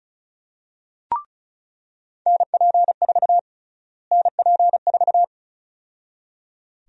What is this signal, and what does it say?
Morse code, heard as a keyed tone, about 36 wpm, 700 Hz. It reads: NP4 NP4
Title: USA Call Sign Prefixes - Repeated 2x - 35wpm